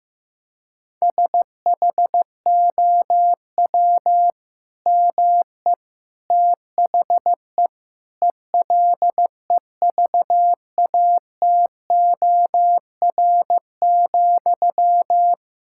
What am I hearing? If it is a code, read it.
Morse code, heard as a keyed tone, about 15 wpm, 700 Hz. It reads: SHOW ME THE ELEVATOR,
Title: Sentences from Top 1600 Words 15wpm